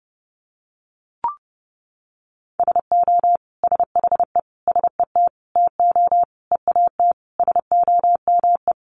Morse code, heard as a keyed tone, about 30 wpm, 700 Hz. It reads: SO SHE SET TO EAT SOME
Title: Sentences from Top 900 Words 30wpm